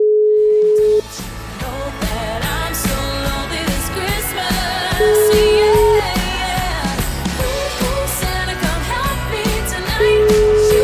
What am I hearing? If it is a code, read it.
Ringback tone (Europe, 425 Hz), with noise.